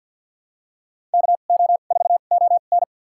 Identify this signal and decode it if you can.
Morse code, heard as a keyed tone, about 52 wpm, spaced as for 37 wpm, 700 Hz. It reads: KO4ON
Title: USA Call Signs - 5 Characters - 2x Character Spacing - 50wpm